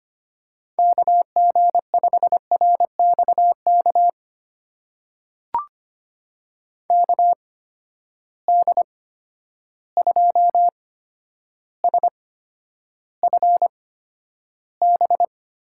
Morse code, heard as a keyed tone, about 25 wpm, spaced as for 11 wpm, 700 Hz. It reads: KG5RXK KD2SFB
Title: USA Call Signs - 6 Characters - 8x Character Spacing - 25wpm